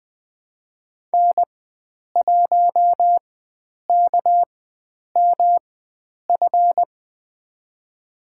Morse code, heard as a keyed tone, about 21 wpm, spaced as for 9 wpm, 700 Hz. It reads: N1KMF
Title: USA Call Signs - 5 Characters - 4x Character Spacing - 20wpm